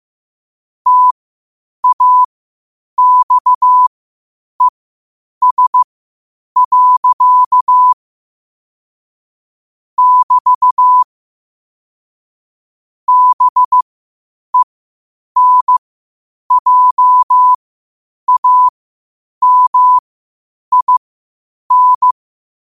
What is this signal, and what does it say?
Morse code, heard as a keyed tone, about 15 wpm, spaced as for 8 wpm, 1000 Hz. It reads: TAXES. <BT> BENJAMIN
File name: qotdfiles / 8wpm